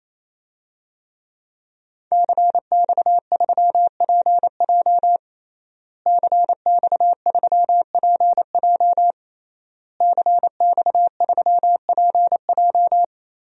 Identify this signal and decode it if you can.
Morse code, heard as a keyed tone, about 28 wpm, 700 Hz. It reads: CX3PJ CX3PJ CX3PJ
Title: DXCC-100 Call Signs - 5 Characters - Repeated 3x - 28wpm